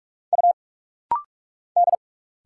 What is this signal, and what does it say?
Morse code, heard as a keyed tone, about 46 wpm, 700 Hz. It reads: U D
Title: Single Letters - Rapid-Fire 45wpm